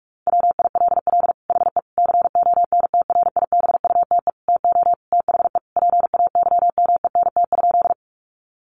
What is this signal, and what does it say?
Morse code, heard as a keyed tone, about 44 wpm, 700 Hz. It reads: WILL HE CONTRIBUTE TO THE PAYMENT?
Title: Sentences from Top 1400 Words 45wpm